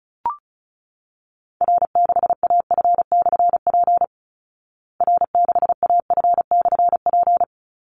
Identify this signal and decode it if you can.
Morse code, heard as a keyed tone, about 34 wpm, 700 Hz. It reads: R6AF/P R6AF/P
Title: Call Signs Repeated 3x - 35wpm